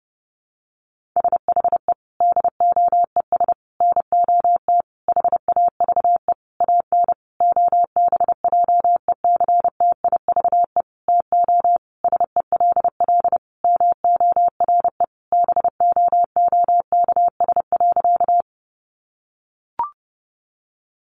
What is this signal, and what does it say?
Morse code, heard as a keyed tone, about 30 wpm, 700 Hz. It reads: SHE DOES NOT HAVE AN OBJECTIVE TO SELL MORE BOOKS.
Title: Sentences from Top 1200 Words 30wpm